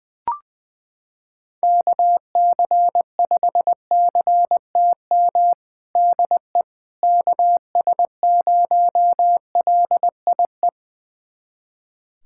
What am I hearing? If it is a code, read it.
Morse code, heard as a keyed tone, about 20 wpm, 700 Hz. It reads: KC5CTM DE KS0LIE
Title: USA Callsign DE Callsign - 6 Characters - 20wpm